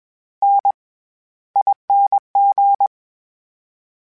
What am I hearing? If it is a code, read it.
Morse code, heard as a keyed tone, about 21 wpm, 800 Hz. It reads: N ING
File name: Special INGs-mp3s / Special INGs-mp3s / SlowToFaster ing @20